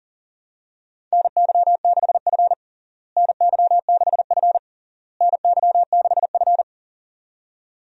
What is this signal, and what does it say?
Morse code, heard as a keyed tone, about 40 wpm, 700 Hz. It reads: NY6F NY6F NY6F
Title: USA Call Signs - 4 Characters - Repeated 3x - 40wpm